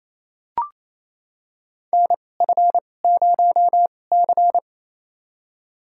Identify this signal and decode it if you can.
Morse code, heard as a keyed tone, about 28 wpm, spaced as for 20 wpm, 700 Hz. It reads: NF0C
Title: USA Call Signs - 4 Characters - 2x Character Spacing - 28wpm